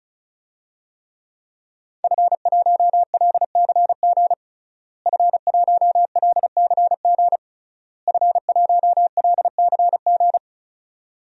Morse code, heard as a keyed tone, about 36 wpm, 700 Hz. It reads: F1LCG F1LCG F1LCG
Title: DXCC-100 Call Signs - 5 Characters - Repeated 3x - 35wpm